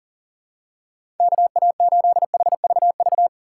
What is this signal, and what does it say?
Morse code, heard as a keyed tone, about 40 wpm, 700 Hz. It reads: KA8HVV
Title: USA Call Signs - 3x Character Spacing - 40wpm